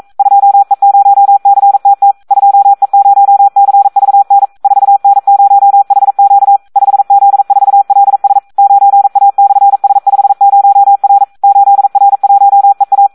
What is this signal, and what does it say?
Morse code, heard as a keyed tone, about 42 wpm, 800 Hz. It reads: 2E0CTT 2E0CVN 4N0HQ 5Z4LS 9A/S50R 8R1EA
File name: Audio practica CW / SD 1 42